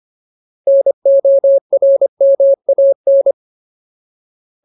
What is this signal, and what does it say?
Morse code, heard as a keyed tone, about 25 wpm, 550 Hz. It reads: NORMAN
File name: Ultimate Word List / 25WPM Ultimate